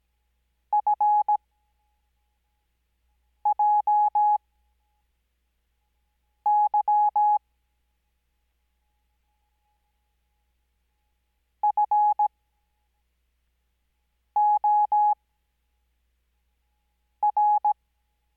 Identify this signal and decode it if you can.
Morse code, heard as a keyed tone, about 17 wpm, spaced as for 4 wpm, 840 Hz. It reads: FJY FOR
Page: Farnsworth Spacing: 15WPM Character speed Spaced at 5WPM.